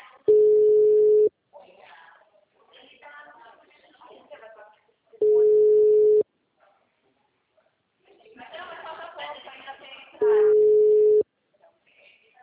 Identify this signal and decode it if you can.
Ringback tone (Europe, 425 Hz), with speech.